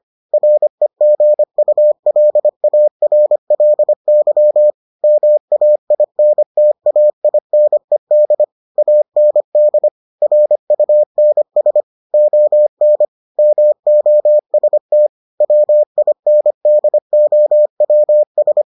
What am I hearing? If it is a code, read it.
Morse code, heard as a keyed tone, about 25 wpm, 600 Hz. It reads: REGULARLY MAINTAINED AND RUNS ON MOST WINDOWS